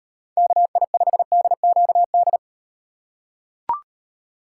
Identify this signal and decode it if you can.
Morse code, heard as a keyed tone, about 38 wpm, 700 Hz. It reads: KI5DQD